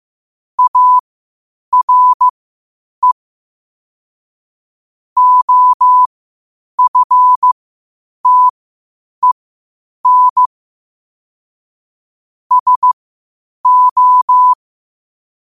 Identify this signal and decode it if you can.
Morse code, heard as a keyed tone, about 15 wpm, spaced as for 8 wpm, 1000 Hz. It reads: ARE OFTEN SO
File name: qotdfiles / 8wpm